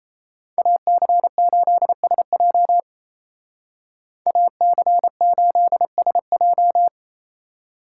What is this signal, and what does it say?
Morse code, heard as a keyed tone, about 33 wpm, 700 Hz. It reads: AC8SJ AC8SJ
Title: Speed Racing - USA 2x2 Call Signs 28wpm